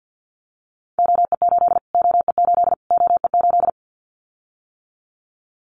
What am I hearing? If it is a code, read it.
Morse code, heard as a keyed tone, about 48 wpm, 700 Hz. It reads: OE8 OE8 OE8
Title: Callsign Prefixes for Top 25 DXCC - Repeated 3x - 50wpm